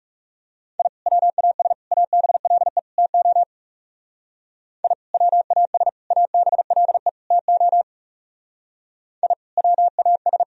Morse code, heard as a keyed tone, about 38 wpm, 700 Hz. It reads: I WAS ABLE TO I WAS ABLE TO I WAS